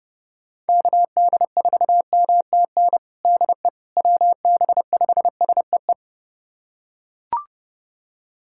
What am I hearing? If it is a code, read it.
Morse code, heard as a keyed tone, about 30 wpm, 700 Hz. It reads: KD4MTN DE WB5SEE
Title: USA Callsign DE Callsign - 6 Characters - 30wpm